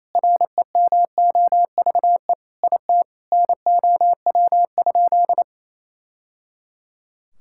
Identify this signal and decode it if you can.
Morse code, heard as a keyed tone, about 27 wpm, 700 Hz. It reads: REMOVE IT NOW?
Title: Sentences from Top 1000 Words - Review Part 5 - 28wpm